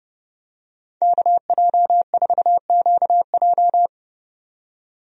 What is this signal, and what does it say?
Morse code, heard as a keyed tone, about 30 wpm, 700 Hz. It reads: KJ4QJ
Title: USA Call Signs - 5 Characters - 30wpm